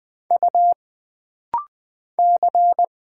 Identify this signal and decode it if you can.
Morse code, heard as a keyed tone, about 20 wpm, 700 Hz. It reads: U C
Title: Single Letter-Number Alt - Rapid-Fire 20wpm